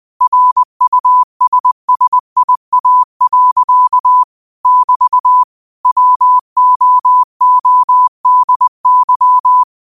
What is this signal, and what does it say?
Morse code, heard as a keyed tone, about 20 wpm, 1000 Hz. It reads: RUSSIA. <BT> WOODY
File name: qotdfiles / 20wpm